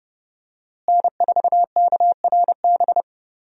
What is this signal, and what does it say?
Morse code, heard as a keyed tone, about 29 wpm, 700 Hz. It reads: N4KRB
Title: USA Call Signs - 5 Characters - Repeated 2x - 30wpm